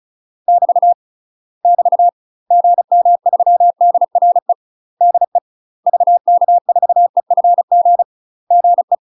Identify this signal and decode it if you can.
Morse code, heard as a keyed tone, about 34 wpm, 700 Hz. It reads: <BT> <BT> GM3DRE DE VK4EFG GE
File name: UK MASTER QSOs TRAINING / 35WPM / UK MASTER QSOs 0000